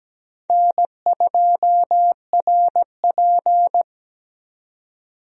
Morse code, heard as a keyed tone, about 17 wpm, 700 Hz. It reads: N2RP
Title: USA Call Signs - 4 Characters - Repeated 3x - 17wpm